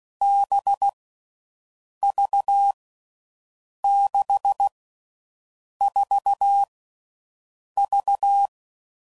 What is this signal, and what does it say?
Morse code, heard as a keyed tone, about 16 wpm, 800 Hz. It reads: B V 6 4 V
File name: Groups / Basic Letter Groups / VB46